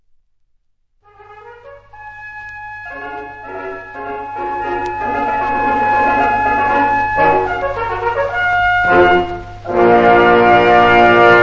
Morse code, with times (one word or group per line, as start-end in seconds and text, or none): none